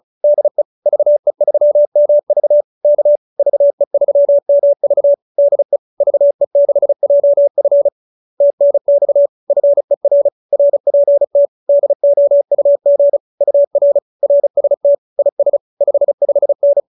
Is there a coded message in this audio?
Yes: Morse code from 0.2 to 16.8 s, heard as a keyed tone, about 35 wpm, 570 Hz. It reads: DE VE3MV K VE3MV DE VE6JF TNX FER RPT DOUG UR RST IS 55N